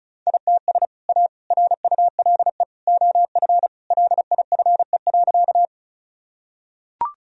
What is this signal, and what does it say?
Morse code, heard as a keyed tone, about 36 wpm, 700 Hz. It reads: ITS A RULE OF LIFE.
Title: Sentences from Top 1000 Words - Review Part 2 - 35wpm